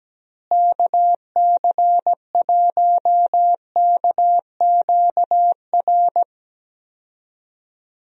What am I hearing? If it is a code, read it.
Morse code, heard as a keyed tone, about 17 wpm, 700 Hz. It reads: KC1KQR
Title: USA Call Signs - 6 Characters - Repeated 2x - 17wpm